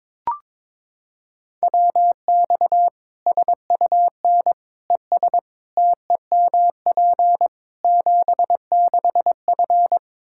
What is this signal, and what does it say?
Morse code, heard as a keyed tone, about 22 wpm, 700 Hz. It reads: WX SUN ES TEMP 76F
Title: Practice QSO - Exchange 2 - Info - 22wpm